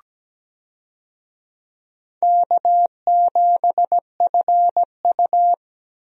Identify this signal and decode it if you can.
Morse code, heard as a keyed tone, about 17 wpm, 700 Hz. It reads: K7FU
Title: USA Call Signs - 4 Characters - 17wpm